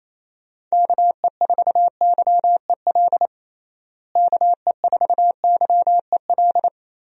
Morse code, heard as a keyed tone, about 28 wpm, 700 Hz. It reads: KE4YEL KE4YEL
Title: USA Call Signs - 6 Characters - Repeated 3x - 28wpm